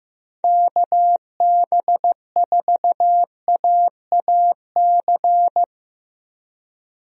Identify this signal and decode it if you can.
Morse code, heard as a keyed tone, about 15 wpm, 700 Hz. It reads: KB4AAC
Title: USA Call Signs - 6 Characters - Repeated 2x - 15wpm